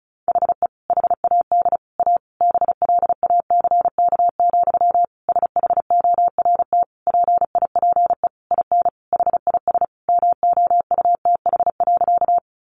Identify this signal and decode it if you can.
Morse code, heard as a keyed tone, about 34 wpm, 700 Hz. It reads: HE HAD A BLACK, SHORT PIPE IN HIS MOUTH.
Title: Sentences from Top 1500 Words 35wpm